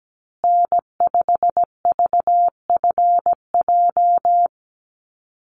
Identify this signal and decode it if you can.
Morse code, heard as a keyed tone, about 17 wpm, 700 Hz. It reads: N5VFJ
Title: USA Call Signs Repeated 3x 17wpm